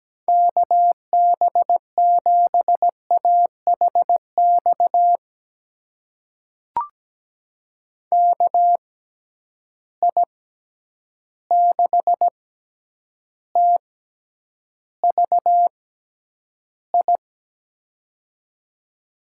Morse code, heard as a keyed tone, about 17 wpm, spaced as for 9 wpm, 700 Hz. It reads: KB7AHX KI6TVI
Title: USA Call Signs - 6 Characters - 6x Character Spacing - 17wpm